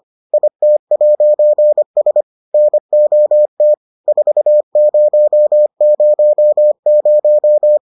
Morse code, heard as a keyed tone, about 25 wpm, 600 Hz. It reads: IT'S NOT 4000